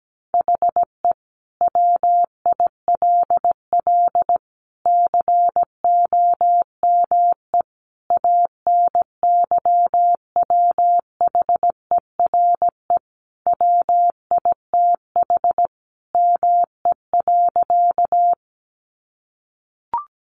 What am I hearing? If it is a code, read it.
Morse code, heard as a keyed tone, about 17 wpm, 700 Hz. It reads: HE WILL COME ANYWHERE WITH ME.
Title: Sentences from Top 1400 Words 17wpm